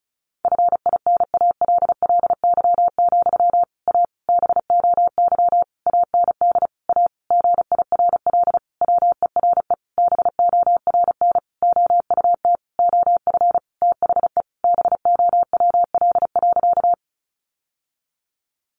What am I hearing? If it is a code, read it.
Morse code, heard as a keyed tone, about 34 wpm, 700 Hz. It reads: FINALLY, A BOY AND A GIRL WERE BORN OUT OF THE BOWL.
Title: Sentences from Top 1200 Words 35wpm